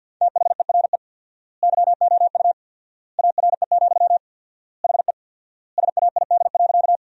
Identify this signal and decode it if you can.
Morse code, heard as a keyed tone, about 50 wpm, 700 Hz. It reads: THERE YOU ARE, HE SAID.
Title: 4x Word-Spacing - Sentences from Top 100 Words - 50wpm